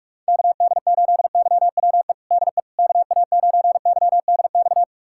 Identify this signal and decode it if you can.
Morse code, heard as a keyed tone, about 42 wpm, 700 Hz. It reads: KD8YWE DE KA9YDX
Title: USA Callsign DE Callsign 45wpm